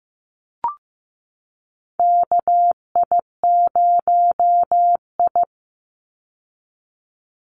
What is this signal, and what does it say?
Morse code, heard as a keyed tone, about 15 wpm, 700 Hz. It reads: KI0I
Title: Call Signs Repeated 2x - 15wpm